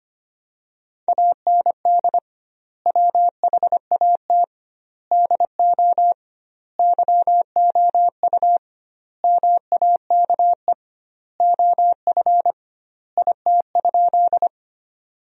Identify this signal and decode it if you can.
Morse code, heard as a keyed tone, about 25 wpm, 700 Hz. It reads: AND WHAT DO YOU MAKE OF IT?
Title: 2x Word-Spacing - Sentences from Top 100 Words - 25wpm